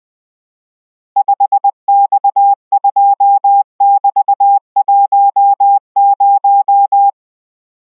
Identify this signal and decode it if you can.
Morse code, heard as a keyed tone, about 20 wpm, 800 Hz. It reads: 5X2<BT>10